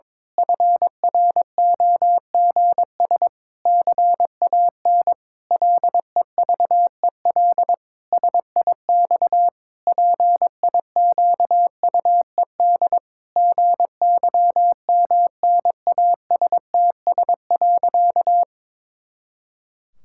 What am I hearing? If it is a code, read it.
Morse code, heard as a keyed tone, about 22 wpm, 700 Hz. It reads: FROGS CAN LEVEL SIX PIQUED GYMNASTS.
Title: Pangram Challenge v2 - 22wpm